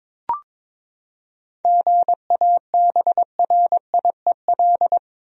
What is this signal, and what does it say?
Morse code, heard as a keyed tone, about 22 wpm, 700 Hz. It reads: GABRIEL
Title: USA Common Names - 22wpm